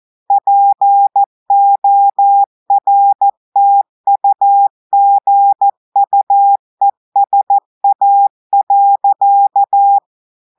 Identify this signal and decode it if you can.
Morse code, heard as a keyed tone, about 14 wpm, 800 Hz. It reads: PORTUGUESA.
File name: audios / portugais / portugais texte en clair 1 vitesse 14 mots minutes